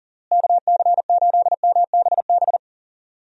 Morse code, heard as a keyed tone, about 37 wpm, 700 Hz. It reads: KC8MBB
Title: USA Call Signs - 6 Characters - Repeated 2x - 40wpm